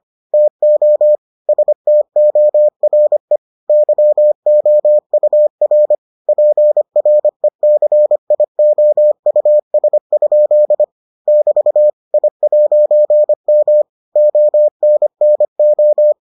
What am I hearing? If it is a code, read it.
Morse code, heard as a keyed tone, about 25 wpm, 600 Hz. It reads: TO STORE YOUR PRECIOUS? <BT> I'M ONNO